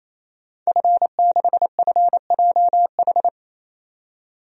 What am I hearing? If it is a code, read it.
Morse code, heard as a keyed tone, about 27 wpm, 700 Hz. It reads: F6FJH
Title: DXCC-100 Call Signs - 5 Characters - 28wpm